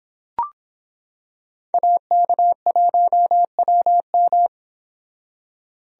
Morse code, heard as a keyed tone, about 25 wpm, 700 Hz. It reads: AK1WM